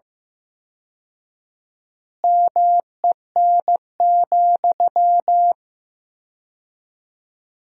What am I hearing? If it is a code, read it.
Morse code, heard as a keyed tone, about 15 wpm, 700 Hz. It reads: MEN,
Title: 4x Word-Spacing - Sentences from Top 500 Words - 15wpm